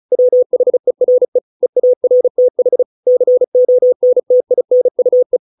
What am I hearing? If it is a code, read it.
Morse code, heard as a keyed tone, about 34 wpm, 500 Hz. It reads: WHERE EARTH CONTINUE